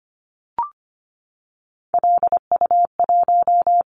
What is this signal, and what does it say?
Morse code, heard as a keyed tone, about 25 wpm, 700 Hz. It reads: LU1
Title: Callsign Prefixes for Top 100 DXCC - 25wpm